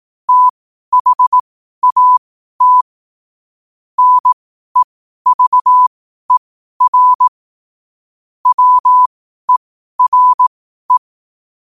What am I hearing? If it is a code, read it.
Morse code, heard as a keyed tone, about 18 wpm, spaced as for 12 wpm, 1000 Hz. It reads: THAT NEVER WERE
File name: qotdfiles / 12wpm